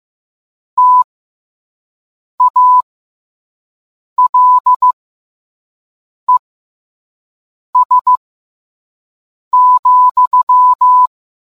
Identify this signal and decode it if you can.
Morse code, heard as a keyed tone, about 15 wpm, spaced as for 5 wpm, 1000 Hz. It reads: TALES,